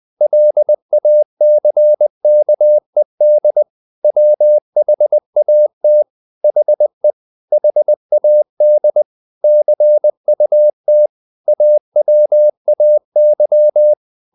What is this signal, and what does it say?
Morse code, heard as a keyed tone, about 20 wpm, 600 Hz. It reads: LACKED WHAT HE HAD CUT AWAY
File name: the old man and the sea part 2 20wpm